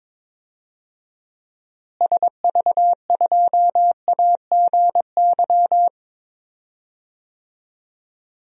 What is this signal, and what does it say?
Morse code, heard as a keyed tone, about 22 wpm, 700 Hz. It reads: SV2AGY
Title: DXCC-100 Call Signs - 6 Characters - Repeated 2x - 22wpm